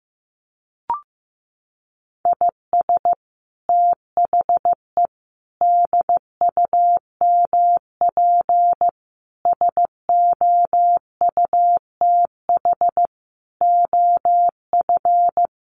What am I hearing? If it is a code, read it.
Morse code, heard as a keyed tone, about 15 wpm, 700 Hz. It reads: IS THE DUMP SOUTH OF
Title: Sentences from Top 1300 Words 15wpm